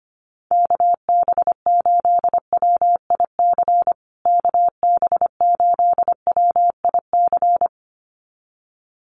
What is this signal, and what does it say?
Morse code, heard as a keyed tone, about 25 wpm, 700 Hz. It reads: KB8WIC KB8WIC
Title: USA Call Signs Repeated 2x 25wpm